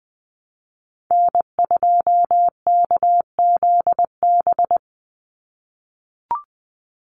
Morse code, heard as a keyed tone, about 20 wpm, 700 Hz. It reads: N2KZB
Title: USA Call Signs 20wpm